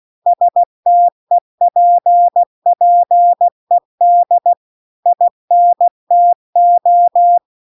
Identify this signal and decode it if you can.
Morse code, heard as a keyed tone, about 16 wpm, 700 Hz. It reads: STEPPED INTO